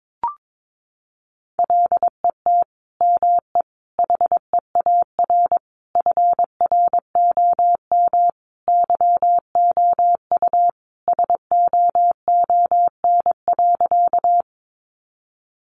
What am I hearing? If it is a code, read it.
Morse code, heard as a keyed tone, about 22 wpm, 700 Hz. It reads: LET ME HEAR FROM YOU SOON.
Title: Sentences from Top 400 Words 22wpm